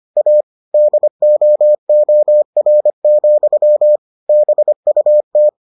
Morse code, heard as a keyed tone, about 25 wpm, 600 Hz. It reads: A DOOR, BUT